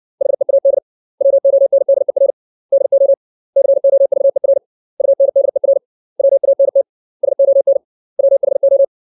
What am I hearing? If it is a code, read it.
Morse code, heard as a keyed tone, about 60 wpm, 550 Hz. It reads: HEAD WONDER DO COVER UNDER WANT SON WHO
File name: 632 words / 632 Words 60WPM